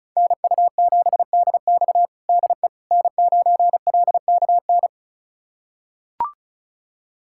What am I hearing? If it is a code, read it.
Morse code, heard as a keyed tone, about 36 wpm, 700 Hz. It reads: NU7DX DE N9LKN
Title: USA Callsign DE Callsign - 5 Characters - 35wpm